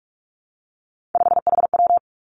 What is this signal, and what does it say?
Morse code, heard as a keyed tone, about 46 wpm, 700 Hz. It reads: 5HW
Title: Three Challenging Letter-Number Combinations 45wpm